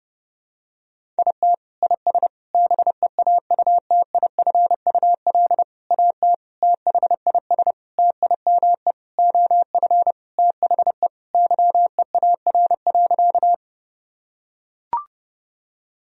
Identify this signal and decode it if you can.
Morse code, heard as a keyed tone, about 30 wpm, 700 Hz. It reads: IT IS BEAUTIFUL AT THIS TIME OF THE YEAR.
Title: Sentences from Top 1000 Words - Review Part 1 - 30wpm